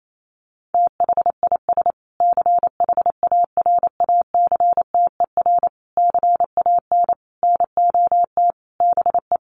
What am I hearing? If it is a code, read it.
Morse code, heard as a keyed tone, about 28 wpm, 700 Hz. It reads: THIS CHARACTER CAN NOT BE
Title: Sets of 5 Words Encore2 28wpm